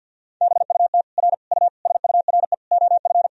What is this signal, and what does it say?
Morse code, heard as a keyed tone, about 52 wpm, 700 Hz. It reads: BUT R U SURE OF
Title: Sentences from Top 300 Words and Abbreviations 50wpm